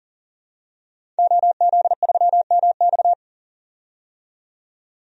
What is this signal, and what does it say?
Morse code, heard as a keyed tone, about 40 wpm, 700 Hz. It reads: OZ3MX
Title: DXCC-100 Call Signs - 5 Characters - 40wpm